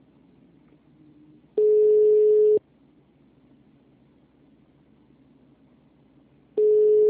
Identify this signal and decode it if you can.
Ringback tone (Europe, 425 Hz).